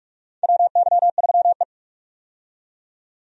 Morse code, heard as a keyed tone, about 45 wpm, 700 Hz. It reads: WY3E
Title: USA Call Signs - 4 Characters - 45wpm